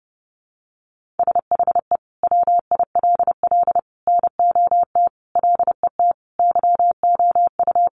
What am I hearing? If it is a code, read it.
Morse code, heard as a keyed tone, about 30 wpm, 700 Hz. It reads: SHE WILL NOT LET YOU
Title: Sentences from Top 1500 Words 30wpm